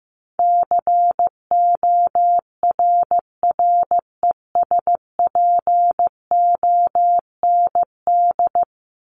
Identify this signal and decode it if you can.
Morse code, heard as a keyed tone, about 15 wpm, 700 Hz. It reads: CORRESPOND
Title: ING Suffix - Top 100 Words with Base Words 15wpm